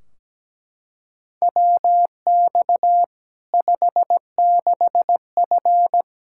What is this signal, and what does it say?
Morse code, heard as a keyed tone, about 17 wpm, 700 Hz. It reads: WX 56F
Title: Practice QSO - Exchange 2 - Info - 17wpm